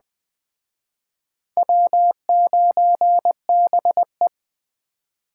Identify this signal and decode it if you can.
Morse code, heard as a keyed tone, about 20 wpm, 700 Hz. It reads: W9BE